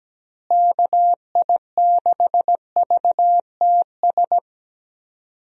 Morse code, heard as a keyed tone, about 17 wpm, 700 Hz. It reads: KI6VTS
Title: USA Call Signs - 6 Characters - Repeated 2x - 17wpm